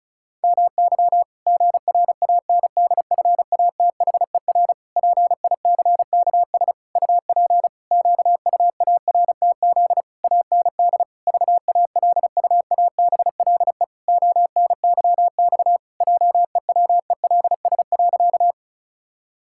Morse code, heard as a keyed tone, about 36 wpm, 700 Hz. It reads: MY GRANDFATHER PICKS UP QUARTZ AND VALUABLE ONYX JEWELS.
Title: Pangram Challenge v2 - 35wpm